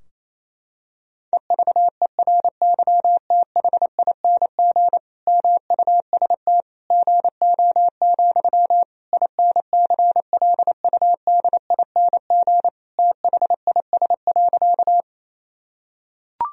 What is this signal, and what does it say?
Morse code, heard as a keyed tone, about 28 wpm, 700 Hz. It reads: EVERYTHING MUST GO, INCLUDING THIS.
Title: Sentences from Top 1000 Words - Review Part 4 - 28wpm